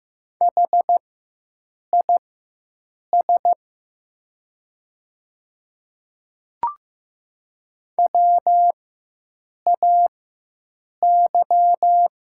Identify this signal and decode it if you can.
Morse code, heard as a keyed tone, about 15 wpm, spaced as for 7 wpm, 700 Hz. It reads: HIS WAY
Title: 4x Character Spacing - Top 100 Words - 15wpm